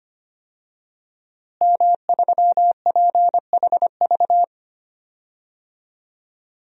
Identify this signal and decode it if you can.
Morse code, heard as a keyed tone, about 25 wpm, 700 Hz. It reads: M3PHV
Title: DXCC-100 Call Signs - 5 Characters - Repeated 2x - 25wpm